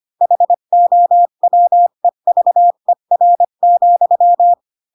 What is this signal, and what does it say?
Morse code, heard as a keyed tone, about 25 wpm, 700 Hz. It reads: HOWEVER,
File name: Pinocchio / Pinocchio ch4 20 25 ik1hge